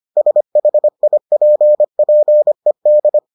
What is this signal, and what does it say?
Morse code, heard as a keyed tone, about 25 wpm, 600 Hz. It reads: SHIPPED